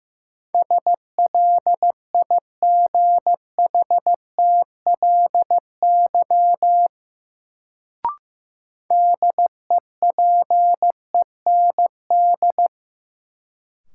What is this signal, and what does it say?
Morse code, heard as a keyed tone, about 15 wpm, 700 Hz. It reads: SLIGHTLY DEPEND
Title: Sentences from Top 1300 Words - New Word Drill - 15wpm